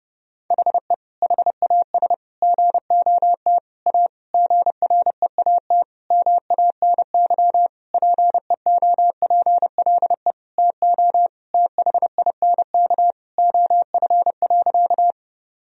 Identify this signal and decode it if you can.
Morse code, heard as a keyed tone, about 30 wpm, 700 Hz. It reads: HE HAS GOT A GREAT MANY PEOPLE TO THINK OF.
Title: Sentences from Top 1000 Words - Review Part 1 - 30wpm